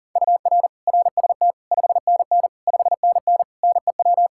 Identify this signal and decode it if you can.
Morse code, heard as a keyed tone, about 37 wpm, 700 Hz. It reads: UR RST 5NN 5NN NEW
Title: Practice QSO - Exchange 1 - Beginning - 40wpm